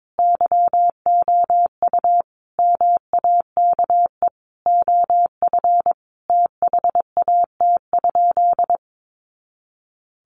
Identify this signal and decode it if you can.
Morse code, heard as a keyed tone, about 22 wpm, 700 Hz. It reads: YOU MAKE OF THAT?
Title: Sentences from Top 100 Words 22wpm